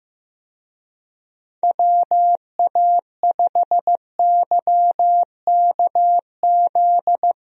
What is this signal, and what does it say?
Morse code, heard as a keyed tone, about 16 wpm, 700 Hz. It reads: WA5YKZ
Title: USA Call Signs - 6 Characters - 15wpm